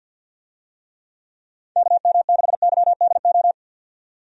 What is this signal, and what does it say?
Morse code, heard as a keyed tone, about 47 wpm, 700 Hz. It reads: KM6YDO
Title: USA Call Signs - 6 Characters - 50wpm